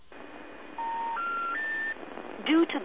Special information tone, with noise.
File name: attflood